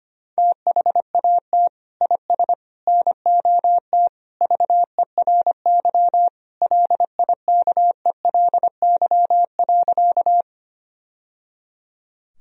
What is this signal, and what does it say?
Morse code, heard as a keyed tone, about 25 wpm, 700 Hz. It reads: THAT IS NOT VERY LIKELY.
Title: Sentences from Top 1000 Words - Review Part 4 - 25wpm